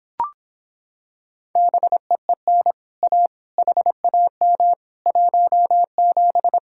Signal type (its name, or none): Morse code